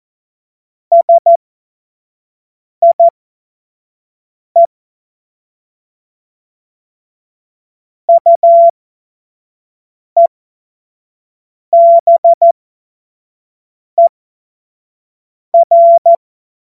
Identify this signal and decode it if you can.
Morse code, heard as a keyed tone, about 14 wpm, spaced as for 5 wpm, 680 Hz. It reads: SIE UEBER
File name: Morsekurs / Witze / Witz 07. Februar 2023 25 F70